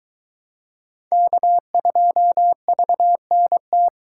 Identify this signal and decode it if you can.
Morse code, heard as a keyed tone, about 22 wpm, 700 Hz. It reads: K2VNT